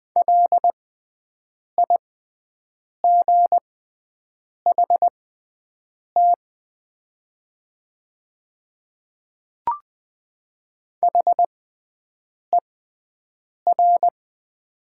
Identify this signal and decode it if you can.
Morse code, heard as a keyed tone, about 20 wpm, spaced as for 7 wpm, 700 Hz. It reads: LIGHT HER
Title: 6x Character Spacing - Top 200 Words - 20wpm